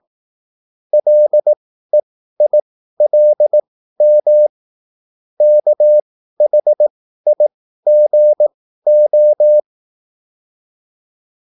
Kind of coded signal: Morse code